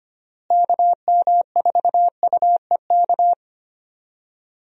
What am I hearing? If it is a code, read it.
Morse code, heard as a keyed tone, about 25 wpm, 700 Hz. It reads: KM4UEK
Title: USA Call Signs - 6 Characters - Repeated 3x - 25wpm